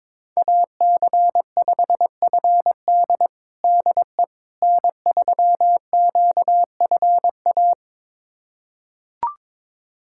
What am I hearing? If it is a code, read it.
Morse code, heard as a keyed tone, about 22 wpm, 700 Hz. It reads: AC5FD DE N3QFA
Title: USA Callsign DE Callsign - 5 Characters - 22wpm